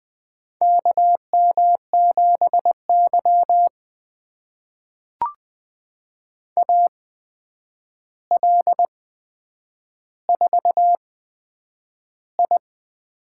Morse code, heard as a keyed tone, about 20 wpm, spaced as for 9 wpm, 700 Hz. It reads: KM7Y AL4I
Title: USA Call Signs - 4 Characters - 8x Character Spacing - 20wpm